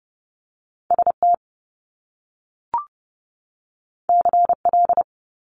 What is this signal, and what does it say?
Morse code, heard as a keyed tone, about 30 wpm, 700 Hz. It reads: ST CL
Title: Top Two Letter Combinations in Words 30wpm